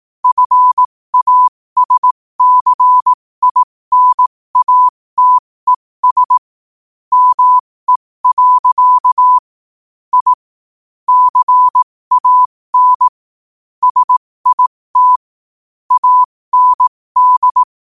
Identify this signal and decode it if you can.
Morse code, heard as a keyed tone, about 18 wpm, spaced as for 15 wpm, 1000 Hz. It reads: FASCINATES ME. I CAN SIT AND
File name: qotdfiles / 15wpm